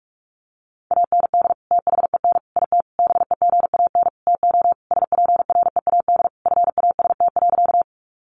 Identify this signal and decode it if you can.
Morse code, heard as a keyed tone, about 44 wpm, 700 Hz. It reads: AND THEN IT BEGAN TO SPREAD FAST.
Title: Sentences from Top 1200 Words 45wpm